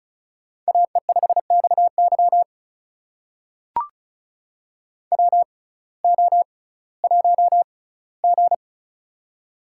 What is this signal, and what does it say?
Morse code, heard as a keyed tone, about 36 wpm, spaced as for 19 wpm, 700 Hz. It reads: AE5XY WO1G
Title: USA Call Signs - 6x Character Spacing - 35wpm